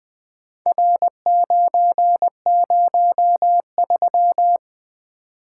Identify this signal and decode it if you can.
Morse code, heard as a keyed tone, about 20 wpm, 700 Hz. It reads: R903
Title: Four Letter-Number Combinations - 20wpm